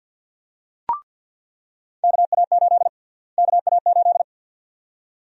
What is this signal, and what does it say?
Morse code, heard as a keyed tone, about 48 wpm, 700 Hz. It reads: KA8 KA8
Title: USA Call Sign Prefixes - Repeated 2x - 50wpm